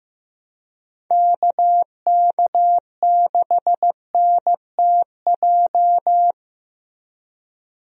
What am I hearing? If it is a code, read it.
Morse code, heard as a keyed tone, about 15 wpm, 700 Hz. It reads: KK6NTJ
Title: USA Call Signs - 6 Characters - Repeated 3x - 15wpm